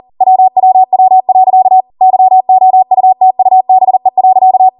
Morse code, heard as a keyed tone, about 40 wpm, 750 Hz. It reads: WWW. YOUTUBE.